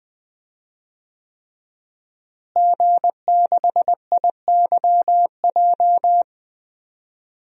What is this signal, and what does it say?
Morse code, heard as a keyed tone, about 20 wpm, 700 Hz. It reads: G6IYJ
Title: DXCC-100 Call Signs - 5 Characters - Repeated 3x - 20wpm